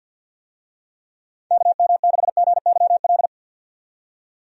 Morse code, heard as a keyed tone, about 52 wpm, 700 Hz. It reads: KM6GYL
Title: USA Call Signs - 6 Characters - 50wpm